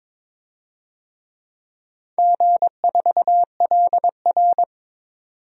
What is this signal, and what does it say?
Morse code, heard as a keyed tone, about 22 wpm, 700 Hz. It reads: G4LR
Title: DXCC-100 Call Signs - 4 Characters - 22wpm